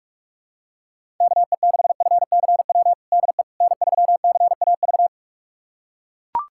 Morse code, heard as a keyed tone, about 41 wpm, 700 Hz. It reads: KE6FCW DE N3CAV
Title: USA Callsign DE Callsign 45wpm